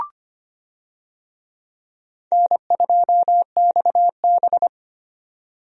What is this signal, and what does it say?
Morse code, heard as a keyed tone, about 25 wpm, 700 Hz. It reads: N2XB
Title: USA Call Signs - 4 Characters - 25wpm